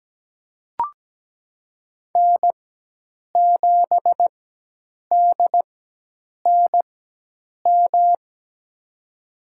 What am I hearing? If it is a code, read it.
Morse code, heard as a keyed tone, about 17 wpm, spaced as for 8 wpm, 700 Hz. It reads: N7DNM
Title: USA Call Signs - 5 Characters - 4x Character Spacing - 17wpm